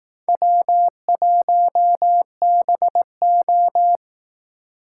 Morse code, heard as a keyed tone, about 18 wpm, 700 Hz. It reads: W1BO
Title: Speed Racing - USA Call Signs 15wpm